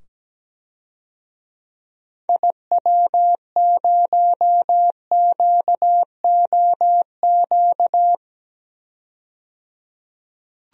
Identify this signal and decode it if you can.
Morse code, heard as a keyed tone, about 17 wpm, 700 Hz. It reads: IW0QOQ
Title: DXCC-100 Call Signs - 6 Characters - 17wpm